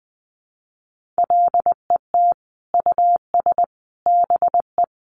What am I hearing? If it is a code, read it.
Morse code, heard as a keyed tone, about 20 wpm, 700 Hz. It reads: LET US BE
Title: Sentences from Top 1400 Words 20wpm